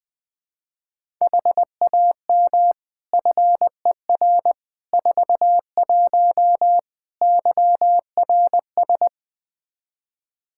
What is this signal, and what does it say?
Morse code, heard as a keyed tone, about 20 wpm, 700 Hz. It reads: HAM FER 41 YRS
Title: Practice QSO - Exchange 3 - More Info - 20wpm